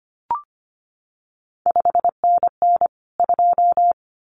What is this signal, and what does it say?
Morse code, heard as a keyed tone, about 25 wpm, 700 Hz. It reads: 5NN 2
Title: CQ WW DX Contest Exchange - 25wpm